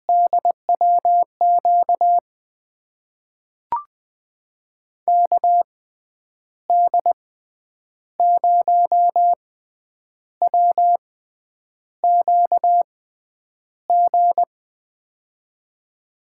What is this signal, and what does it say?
Morse code, heard as a keyed tone, about 20 wpm, spaced as for 8 wpm, 700 Hz. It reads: DWQ KD0WQG
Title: USA Call Signs - 6 Characters - 6x Character Spacing - 20wpm